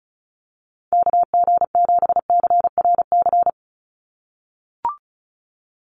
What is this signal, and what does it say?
Morse code, heard as a keyed tone, about 36 wpm, 700 Hz. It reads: KG7CRC
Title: USA Call Signs 35wpm